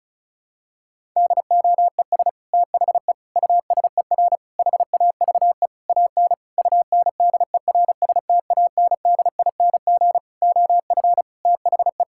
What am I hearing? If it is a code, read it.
Morse code, heard as a keyed tone, about 34 wpm, 700 Hz. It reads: DOES THE USER HAVE AN UNDERSTANDING OF THE
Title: Sentences from Top 1000 Words - Review Part 5 - 35wpm